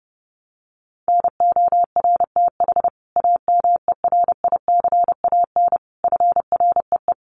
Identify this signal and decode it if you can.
Morse code, heard as a keyed tone, about 28 wpm, 700 Hz. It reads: NORTH AMERICAN FREE
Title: Sets of 5 Words Encore 30wpm